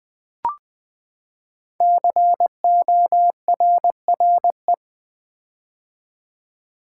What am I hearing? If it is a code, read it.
Morse code, heard as a keyed tone, about 21 wpm, 700 Hz. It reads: CORRE
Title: Build A Word - 5 Letters At A Time - 10-Letter Words - 20wpm